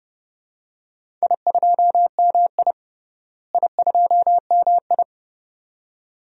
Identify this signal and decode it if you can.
Morse code, heard as a keyed tone, about 30 wpm, 700 Hz. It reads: I2MI I2MI
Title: DXCC-100 Call Signs - 4 Characters - Repeated 2x - 30wpm